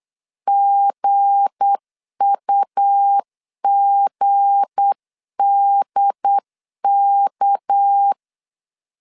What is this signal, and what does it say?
Morse code, heard as a keyed tone, about 8 wpm, 800 Hz. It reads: GUGDK
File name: LEZIONI / 7 GF9Z / 7 G 8wpm 3118